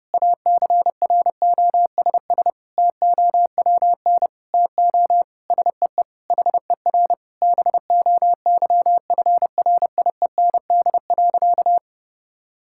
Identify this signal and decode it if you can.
Morse code, heard as a keyed tone, about 32 wpm, 700 Hz. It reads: ACROSS TOWN TO SEE HER BOYFRIEND.
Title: Sentences from Top 1000 Words - Review Part 6 - 30wpm